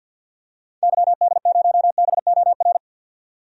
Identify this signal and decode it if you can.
Morse code, heard as a keyed tone, about 52 wpm, 700 Hz. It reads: YD0BOR
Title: DXCC-100 Call Signs - 6 Characters - 50wpm